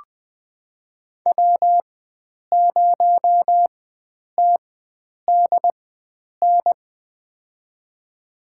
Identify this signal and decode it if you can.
Morse code, heard as a keyed tone, about 20 wpm, spaced as for 9 wpm, 700 Hz. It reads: W0TDN
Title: USA Call Signs - 5 Characters - 4x Character Spacing - 20wpm